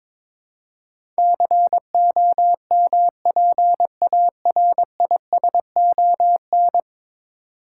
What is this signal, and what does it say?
Morse code, heard as a keyed tone, about 22 wpm, 700 Hz. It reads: COMPARISON
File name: 10 Letter Words - First 3 Letters Given - 22wpm